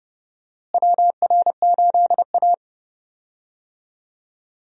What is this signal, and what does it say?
Morse code, heard as a keyed tone, about 30 wpm, 700 Hz. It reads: WR8A
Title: USA Call Signs - 4 Characters - Repeated 2x - 30wpm